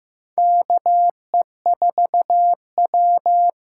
Morse code, heard as a keyed tone, about 15 wpm, 700 Hz. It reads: KE4W
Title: USA Call Signs - 6 Characters - 8x Character Spacing - 15wpm